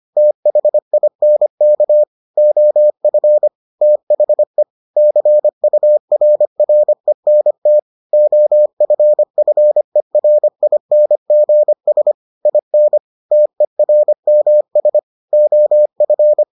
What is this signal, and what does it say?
Morse code, heard as a keyed tone, about 25 wpm, 600 Hz. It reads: THINK OF THE CURRENT OFFERINGS IN TERMS OF